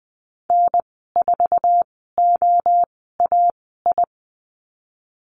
Morse code, heard as a keyed tone, about 19 wpm, spaced as for 14 wpm, 700 Hz. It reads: N4OAI
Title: 055 Lesson 10g - Letter r - Call Signs 20wpm